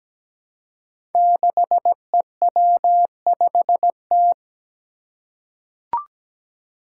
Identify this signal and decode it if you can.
Morse code, heard as a keyed tone, about 17 wpm, 700 Hz. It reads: 6EW5T
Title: Five Challenging Letter-Number Combinations 17wpm